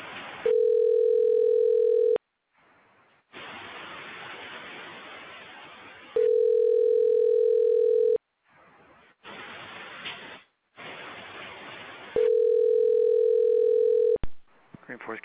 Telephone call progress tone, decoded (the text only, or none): ringback tone